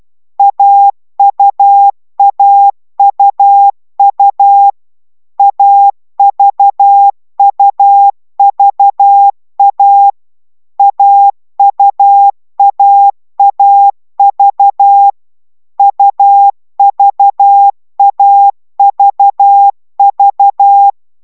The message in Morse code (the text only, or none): AUAUU AVUVA AUAAV UVAVV